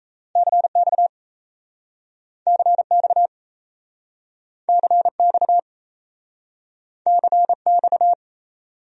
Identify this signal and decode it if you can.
Morse code, heard as a keyed tone, about 39 wpm, 700 Hz. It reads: CX CX CX CX